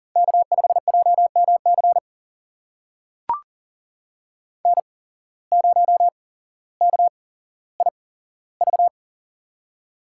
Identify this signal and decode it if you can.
Morse code, heard as a keyed tone, about 40 wpm, spaced as for 17 wpm, 700 Hz. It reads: K5JMC N0KIV
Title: USA Call Signs - 8x Character Spacing - 40wpm